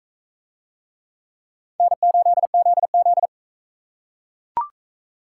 Morse code, heard as a keyed tone, about 43 wpm, 700 Hz. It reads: N8ZZ